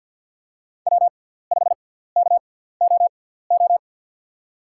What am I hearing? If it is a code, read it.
Morse code, heard as a keyed tone, about 49 wpm, spaced as for 17 wpm, 690 Hz. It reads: W5KOO
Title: USA Call Signs - 5 Characters - 6x Character Spacing - 50wpm